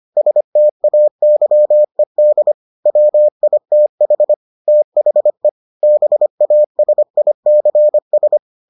Morse code, heard as a keyed tone, about 25 wpm, 600 Hz. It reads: STAYED WITH THE BASICS